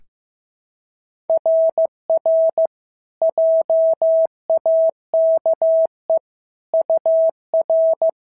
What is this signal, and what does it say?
Morse code, heard as a keyed tone, about 14 wpm, 650 Hz. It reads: RR JAKE UR
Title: SOTA QSO Practice 15wpm